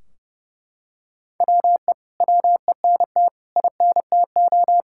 Morse code, heard as a keyed tone, about 30 wpm, 700 Hz. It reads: WE WENT INTO
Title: Sentences from Top 1600 Words 30wpm